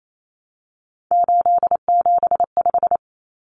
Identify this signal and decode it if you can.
Morse code, heard as a keyed tone, about 28 wpm, 700 Hz. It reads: 875
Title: Numbers - 3-Digit 28wpm